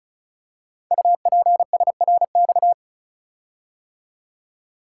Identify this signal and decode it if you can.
Morse code, heard as a keyed tone, about 36 wpm, 700 Hz. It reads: UPSRX
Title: Code Groups - 5 Letters - 35wpm